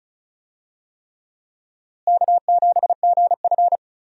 Morse code, heard as a keyed tone, about 36 wpm, 700 Hz. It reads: K7GF
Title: USA Call Signs - 4 Characters - Repeated 3x - 35wpm